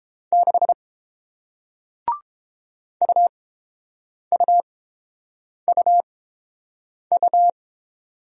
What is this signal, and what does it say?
Morse code, heard as a keyed tone, about 30 wpm, spaced as for 7 wpm, 700 Hz. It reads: 6 UUUU